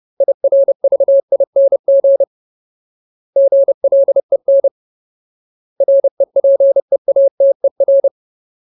Morse code, heard as a keyed tone, about 30 wpm, 550 Hz. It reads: IRVING GLEN REPEATER
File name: Ultimate Word List / 30WPM Ultimate